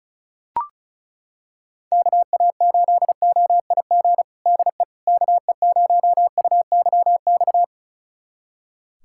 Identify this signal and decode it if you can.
Morse code, heard as a keyed tone, about 36 wpm, 700 Hz. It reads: KA8OIG DE KE0UYX
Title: USA Callsign DE Callsign - 6 Characters - 35wpm